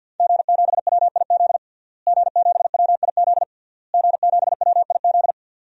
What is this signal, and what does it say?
Morse code, heard as a keyed tone, about 51 wpm, 700 Hz. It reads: G7WIZ G7WIZ G7WIZ
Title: DXCC-100 Call Signs - 5 Characters - Repeated 3x - 50wpm